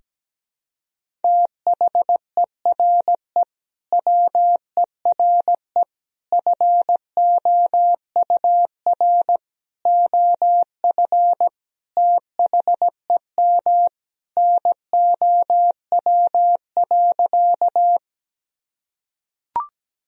Morse code, heard as a keyed tone, about 17 wpm, 700 Hz. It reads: THERE WERE FOUR OF THEM NOW.
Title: Sentences from Top 1000 Words - Review Part 1 - 17wpm